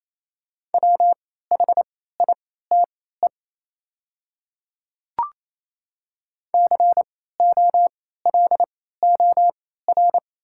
Morse code, heard as a keyed tone, about 28 wpm, spaced as for 16 wpm, 700 Hz. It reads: WHITE COLOR
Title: 3x Character Spacing - Top 500 Words - 28wpm